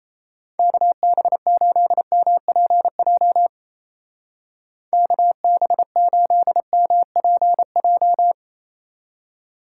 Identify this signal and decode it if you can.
Morse code, heard as a keyed tone, about 35 wpm, 700 Hz. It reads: KB8MPJ KB8MPJ